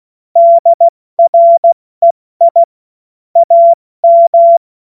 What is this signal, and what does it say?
Morse code, heard as a keyed tone, about 16 wpm, 680 Hz. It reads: DREI AM
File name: Witz-23-September-2025-70-F80